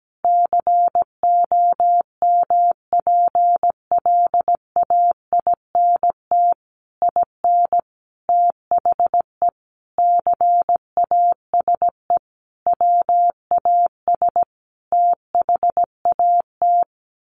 Morse code, heard as a keyed tone, about 17 wpm, 700 Hz. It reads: COMPLAINT IN THE CASE WAS THAT
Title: Sentences from Top 1400 Words 17wpm